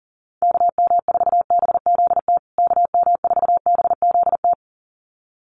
Morse code, heard as a keyed tone, about 40 wpm, 700 Hz. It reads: KM4BZT KM4BZT
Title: USA Call Signs Repeated 2x 40wpm